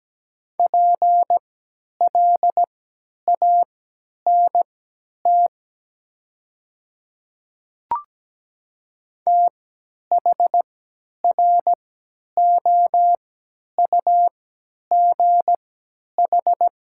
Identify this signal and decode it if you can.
Morse code, heard as a keyed tone, about 17 wpm, spaced as for 10 wpm, 700 Hz. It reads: PLANT THROUGH
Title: 3x Character Spacing - Top 300 Words - 17wpm